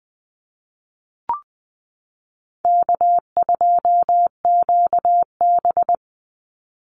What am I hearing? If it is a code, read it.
Morse code, heard as a keyed tone, about 20 wpm, 700 Hz. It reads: K2QB
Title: Call Signs Repeated 2x - 20wpm